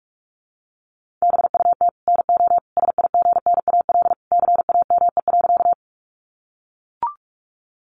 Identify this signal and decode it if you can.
Morse code, heard as a keyed tone, about 44 wpm, 700 Hz. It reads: BUT NO SIGNAL CAME.
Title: Sentences from Top 1400 Words 45wpm